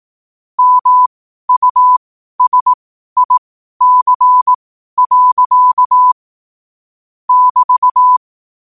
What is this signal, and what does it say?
Morse code, heard as a keyed tone, about 18 wpm, spaced as for 12 wpm, 1000 Hz. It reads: MUSIC. <BT>
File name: qotdfiles / 12wpm